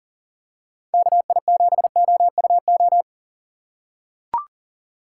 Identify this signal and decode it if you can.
Morse code, heard as a keyed tone, about 40 wpm, 700 Hz. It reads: KI7OUO
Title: USA Call Signs - 6x Character Spacing - 40wpm